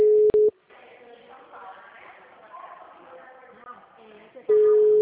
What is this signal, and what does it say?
Ringback tone (Europe, 425 Hz), with speech.